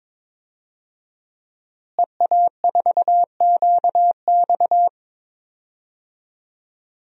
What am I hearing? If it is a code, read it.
Morse code, heard as a keyed tone, about 22 wpm, 700 Hz. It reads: EA4QX
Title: DXCC-100 Call Signs - 5 Characters - Repeated 3x - 22wpm